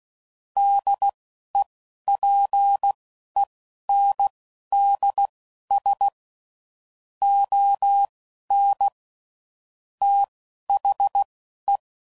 Morse code, heard as a keyed tone, about 16 wpm, spaced as for 11 wpm, 800 Hz. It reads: DEPENDS ON THE
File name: CW Code Groups @10WPM / Technical Subjects / SwPwrSup